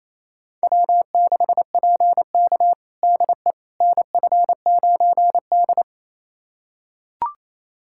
Morse code, heard as a keyed tone, about 28 wpm, 700 Hz. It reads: W6PK DE NF9D
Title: USA Callsign DE Callsign - 4 Characters - 28wpm